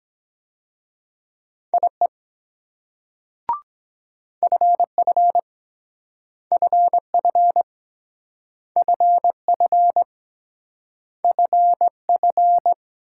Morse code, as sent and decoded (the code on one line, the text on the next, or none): .. / ..-. ..-. / ..-. ..-. / ..-. ..-. / ..-. ..-.
IE FF FF FF FF